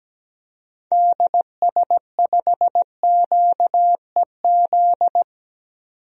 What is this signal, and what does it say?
Morse code, heard as a keyed tone, about 17 wpm, 700 Hz. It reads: DS5QEZ
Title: DXCC-100 Call Signs - 6 Characters - 17wpm